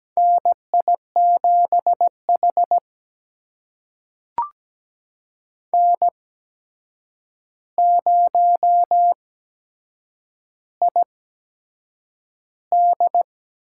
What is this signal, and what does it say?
Morse code, heard as a keyed tone, about 17 wpm, spaced as for 7 wpm, 700 Hz. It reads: NI7H N0ID
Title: USA Call Signs - 4 Characters - 8x Character Spacing - 17wpm